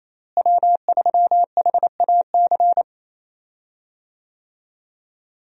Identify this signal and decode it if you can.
Morse code, heard as a keyed tone, about 28 wpm, 700 Hz. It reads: W3HAC
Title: USA Call Signs - 5 Characters - Repeated 3x - 28wpm